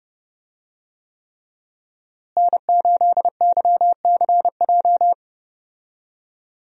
Morse code, heard as a keyed tone, about 32 wpm, 700 Hz. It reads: N8YCJ
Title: USA Call Signs - 5 Characters - Repeated 3x - 30wpm